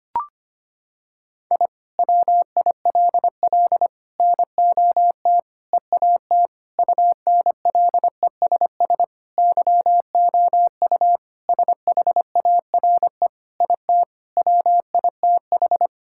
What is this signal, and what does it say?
Morse code, heard as a keyed tone, about 25 wpm, 700 Hz. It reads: I WILL NOT EAT UNLESS YOU SHARE IT WITH
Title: Sentences from Top 1000 Words - Review Part 6 - 25wpm